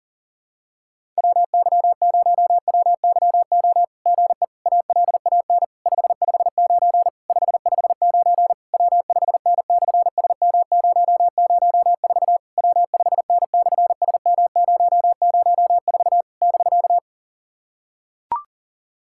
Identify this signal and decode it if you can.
Morse code, heard as a keyed tone, about 41 wpm, 700 Hz. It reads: WY0WYO GE ALAN 559 559 W5N/SM004 W5N/SM004 <BK>
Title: SOTA QSO Practice 40wpm